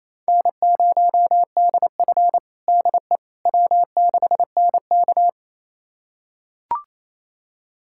Morse code, heard as a keyed tone, about 28 wpm, 700 Hz. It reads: N0DF DE W6NK
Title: USA Callsign DE Callsign - 4 Characters - 28wpm